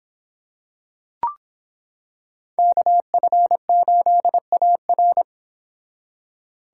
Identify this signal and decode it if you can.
Morse code, heard as a keyed tone, about 26 wpm, 700 Hz. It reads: KF8AR